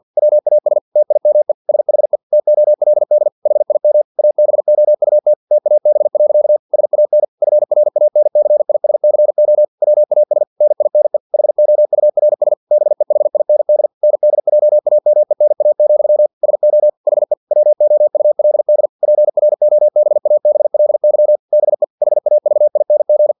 Morse code, heard as a keyed tone, about 50 wpm, 600 Hz. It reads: WAS TIME SHE TOLD HIM ABOUT TAB. SAN FRANCISCO WAS NINE HOURS BEHIND NDJAMENA, SO HE WOULD PROBABLY BE HAVING
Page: Never 49wpm